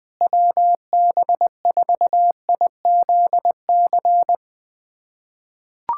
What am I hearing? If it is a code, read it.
Morse code, heard as a keyed tone, about 20 wpm, 700 Hz. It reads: WB4IZC
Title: USA Callsign DE Callsign - 6 Characters - 20wpm